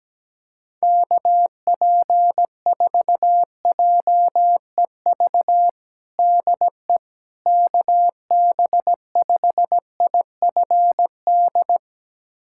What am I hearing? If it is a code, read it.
Morse code, heard as a keyed tone, about 17 wpm, 700 Hz. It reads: KP4JEV DE KB5IFD
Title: USA Callsign DE Callsign 17wpm